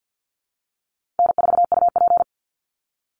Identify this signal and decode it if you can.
Morse code, heard as a keyed tone, about 50 wpm, 700 Hz. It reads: N4UP
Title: Call Signs Repeated 2x - 50wpm